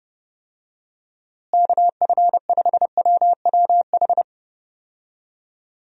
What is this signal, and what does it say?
Morse code, heard as a keyed tone, about 30 wpm, 700 Hz. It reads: KF5WWH
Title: USA Call Signs - 6 Characters - Repeated 2x - 30wpm